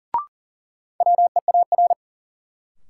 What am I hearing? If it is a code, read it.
Morse code, heard as a keyed tone, about 40 wpm, 700 Hz. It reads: WEAR
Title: Sentences from Top 1500 Words - New Word Drill Alt - 40wpm